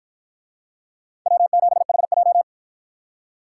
Morse code, heard as a keyed tone, about 53 wpm, 700 Hz. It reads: W7HJ